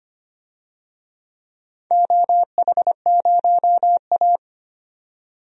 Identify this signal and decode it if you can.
Morse code, heard as a keyed tone, about 25 wpm, 700 Hz. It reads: OH0A
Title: DXCC-100 Call Signs - 4 Characters - 25wpm